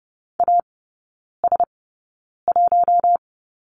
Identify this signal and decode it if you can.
Morse code, heard as a keyed tone, about 30 wpm, spaced as for 9 wpm, 700 Hz. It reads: AS1
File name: Lesson 16e - Letter c - Three Characters 30wpm